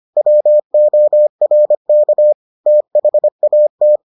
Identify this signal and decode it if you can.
Morse code, heard as a keyed tone, about 25 wpm, 600 Hz. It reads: WORK THAT